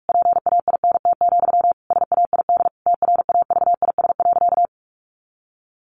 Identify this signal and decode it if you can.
Morse code, heard as a keyed tone, about 44 wpm, 700 Hz. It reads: PAINT, SAID TRAVIS.
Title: A Sound Of Thunder 45wpm